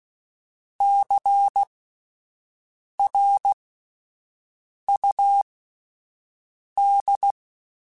Morse code, heard as a keyed tone, about 16 wpm, spaced as for 5 wpm, 800 Hz. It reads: CRUD
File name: Alphabetical / Alphabetical mp3s / Cs @5